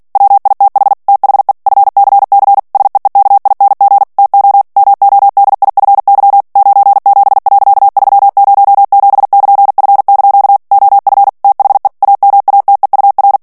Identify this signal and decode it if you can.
Morse code, heard as a keyed tone, about 48 wpm, 800 Hz. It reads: WITH THE FCC SEEKING TO MODIFY 97.307<KN>F) OF THE AMATEUR
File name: ANGLAIS texte clair 3 vitesse 45 mots minute